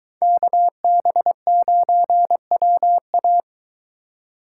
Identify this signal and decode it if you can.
Morse code, heard as a keyed tone, about 23 wpm, 700 Hz. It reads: KB9WA